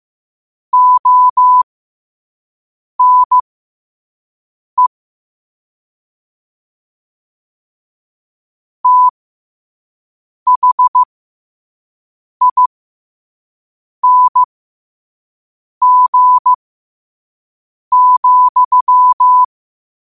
Morse code, heard as a keyed tone, about 15 wpm, spaced as for 5 wpm, 1000 Hz. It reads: ONE THING,
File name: qotdfiles / 5wpm